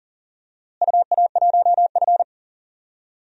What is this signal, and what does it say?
Morse code, heard as a keyed tone, about 40 wpm, 700 Hz. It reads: UA1F
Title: DXCC-100 Call Signs - 4 Characters - Repeated 3x - 40wpm